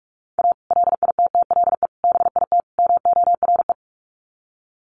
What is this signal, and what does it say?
Morse code, heard as a keyed tone, about 44 wpm, 700 Hz. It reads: A LITTLE BIT MORE
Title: Sets of 4 Words 45wpm